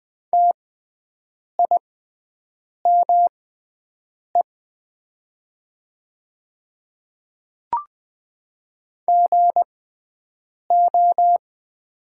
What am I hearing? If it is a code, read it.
Morse code, heard as a keyed tone, about 20 wpm, spaced as for 7 wpm, 700 Hz. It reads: TIME GO
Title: 6x Character Spacing - Top 200 Words - 20wpm